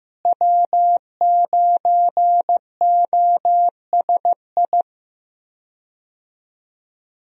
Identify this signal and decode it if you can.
Morse code, heard as a keyed tone, about 15 wpm, 700 Hz. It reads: W9OSI
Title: USA Call Signs - 5 Characters - Repeated 3x - 15wpm